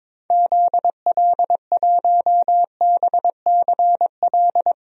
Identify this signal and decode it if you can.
Morse code, heard as a keyed tone, about 22 wpm, 700 Hz. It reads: ZL1BCL
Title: DXCC-100 Call Signs - 6 Characters - 22wpm